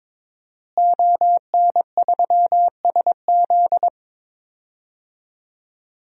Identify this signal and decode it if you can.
Morse code, heard as a keyed tone, about 22 wpm, 700 Hz. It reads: ON3SZ
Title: DXCC-100 Call Signs - 5 Characters - Repeated 3x - 22wpm